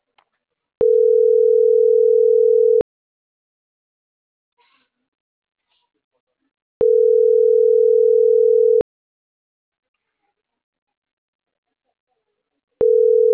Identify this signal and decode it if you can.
Ringback tone (North America), 440 + 480 Hz.